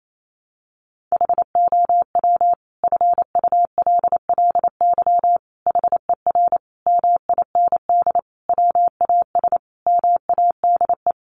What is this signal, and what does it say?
Morse code, heard as a keyed tone, about 28 wpm, 700 Hz. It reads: HOW FULLY HER MIND WAS MADE
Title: Sentences from Top 1300 Words 28wpm